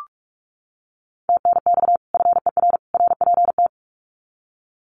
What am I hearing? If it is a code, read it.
Morse code, heard as a keyed tone, about 44 wpm, 700 Hz. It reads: TNX FER RPT
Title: Common QSO Phrases 45wpm